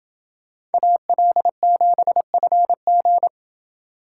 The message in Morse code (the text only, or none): AL7FG